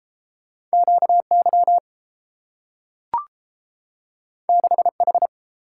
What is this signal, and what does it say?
Morse code, heard as a keyed tone, about 30 wpm, 700 Hz. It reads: QY 6H